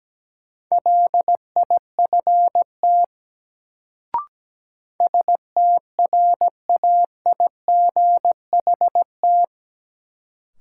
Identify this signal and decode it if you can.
Morse code, heard as a keyed tone, about 17 wpm, 700 Hz. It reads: LIFT STRAIGHT
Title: Sentences from Top 1000 Words - New Word Drill - 17wpm